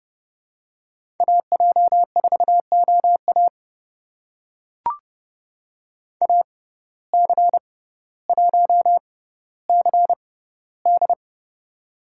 Morse code, heard as a keyed tone, about 32 wpm, spaced as for 16 wpm, 700 Hz. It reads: AJ4OA AC1CD
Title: USA Call Signs - 5 Characters - 6x Character Spacing - 30wpm